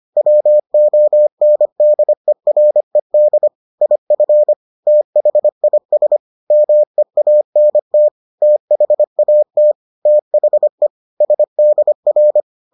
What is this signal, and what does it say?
Morse code, heard as a keyed tone, about 25 wpm, 600 Hz. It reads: WONDERED IF THIS MEANT THAT THE SDR